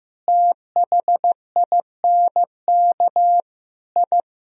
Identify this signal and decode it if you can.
Morse code, heard as a keyed tone, about 15 wpm, 700 Hz. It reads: THINK I
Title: Sentences from Top 1000 Words - Review Part 4 - 15wpm